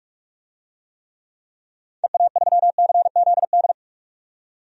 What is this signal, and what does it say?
Morse code, heard as a keyed tone, about 42 wpm, 690 Hz. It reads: EA3CZD
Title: DXCC-100 Call Signs - 6 Characters - 45wpm